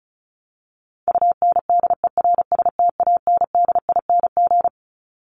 Morse code, heard as a keyed tone, about 34 wpm, 700 Hz. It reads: UNDERSTANDING
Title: ING Suffix - Top 100 Words with Base Words 35wpm